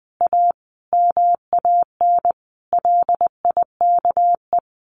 Morse code, heard as a keyed tone, about 20 wpm, 700 Hz. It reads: A MAN LIKE
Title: Sentences from Top 1200 Words 20wpm